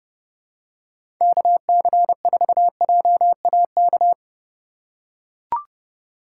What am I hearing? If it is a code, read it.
Morse code, heard as a keyed tone, about 30 wpm, 700 Hz. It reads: KC4JAK
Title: USA Call Signs - 6 Characters - 3x Character Spacing - 30wpm